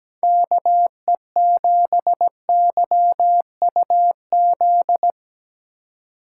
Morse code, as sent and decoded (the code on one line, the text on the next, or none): -.- . --... -.-- ..- --..
KE7YUZ